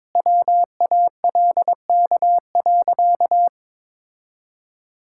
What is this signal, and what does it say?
Morse code, heard as a keyed tone, about 21 wpm, 690 Hz. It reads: WALK.
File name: practice / 22wpm / Easy Sentences with Hints - 22wpm